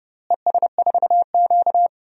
Morse code, heard as a keyed tone, about 32 wpm, 700 Hz. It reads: ES4Q
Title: DXCC-100 Call Signs - 4 Characters - Repeated 2x - 30wpm